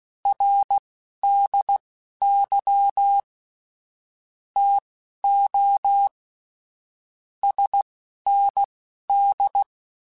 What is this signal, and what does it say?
Morse code, heard as a keyed tone, about 16 wpm, spaced as for 11 wpm, 800 Hz. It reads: RDY TO SND
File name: QSOs / QSOs-mp3s / WA7QOX 1@10